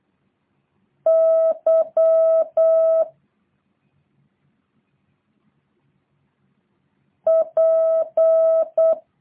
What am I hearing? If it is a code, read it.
Morse code, heard as a keyed tone, about 8 wpm, 650 Hz. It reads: Y P